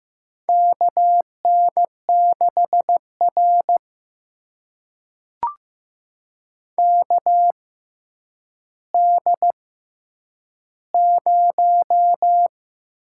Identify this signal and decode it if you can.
Morse code, heard as a keyed tone, about 15 wpm, spaced as for 9 wpm, 700 Hz. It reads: KN6R KD0
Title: USA Call Signs - 6x Character Spacing - 15wpm